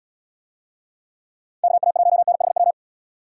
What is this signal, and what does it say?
Morse code, heard as a keyed tone, about 77 wpm, 700 Hz. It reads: KA1NHW